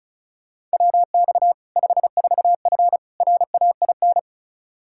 Morse code, heard as a keyed tone, about 34 wpm, 700 Hz. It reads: WX 54F RAIN
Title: Practice QSO - Exchange 2 - Info - 35wpm